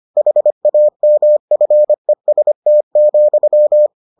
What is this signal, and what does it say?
Morse code, heard as a keyed tone, about 25 wpm, 600 Hz. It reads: HAMFEST,